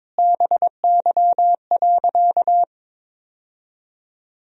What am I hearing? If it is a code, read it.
Morse code, heard as a keyed tone, about 22 wpm, 700 Hz. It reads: BY.
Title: Sentences from Top 1000 Words - Review Part 1 - 22wpm